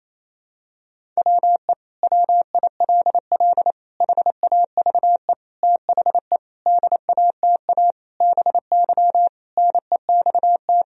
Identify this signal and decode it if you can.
Morse code, heard as a keyed tone, about 28 wpm, 700 Hz. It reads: WE WILL HAVE THE DATA BY NEXT
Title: Sentences from Top 1000 Words - Review Part 3 - 28wpm